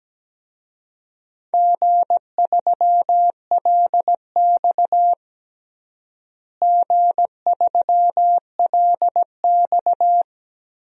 Morse code, heard as a keyed tone, about 17 wpm, 700 Hz. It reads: G3LX G3LX
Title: DXCC-100 Call Signs - 4 Characters - Repeated 3x - 17wpm